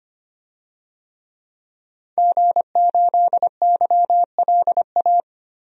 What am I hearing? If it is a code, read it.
Morse code, heard as a keyed tone, about 25 wpm, 700 Hz. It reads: G8YLA
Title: DXCC-100 Call Signs - 5 Characters - 25wpm